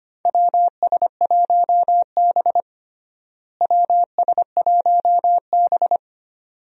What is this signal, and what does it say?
Morse code, heard as a keyed tone, about 25 wpm, 700 Hz. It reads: WS1B WS1B
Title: USA Call Signs - 4 Characters - Repeated 2x - 25wpm